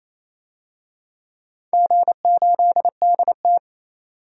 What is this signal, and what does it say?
Morse code, heard as a keyed tone, about 28 wpm, 690 Hz. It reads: G8DT
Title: DXCC-100 Call Signs - 4 Characters - 28wpm